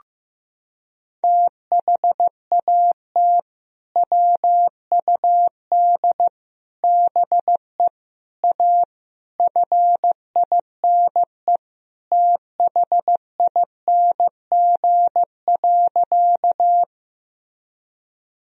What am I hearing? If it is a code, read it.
Morse code, heard as a keyed tone, about 15 wpm, 700 Hz. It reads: THAT WUD BE A FINE THING.
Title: Sentences from Top 500 Words and Abbreviations 15wpm